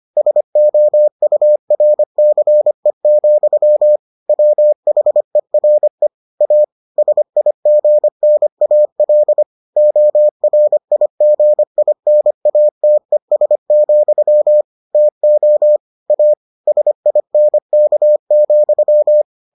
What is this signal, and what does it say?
Morse code, heard as a keyed tone, about 25 wpm, 600 Hz. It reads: SOURCE, WHERE A SIGNAL ORIGINATES, TO A SINK,